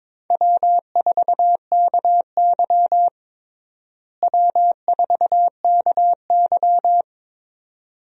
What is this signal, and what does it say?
Morse code, heard as a keyed tone, about 22 wpm, 700 Hz. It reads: W4KY W4KY
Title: USA Call Signs - 4 Characters - Repeated 2x - 22wpm